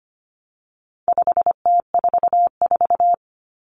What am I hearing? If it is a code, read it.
Morse code, heard as a keyed tone, about 25 wpm, 700 Hz. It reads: 5T44
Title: Common Cut Numbers - 4-Digit 25wpm